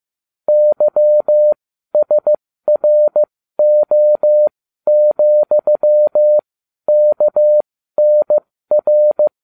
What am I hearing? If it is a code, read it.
Morse code, heard as a keyed tone, about 15 wpm, spaced as for 12 wpm, 600 Hz. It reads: YSRO,KNR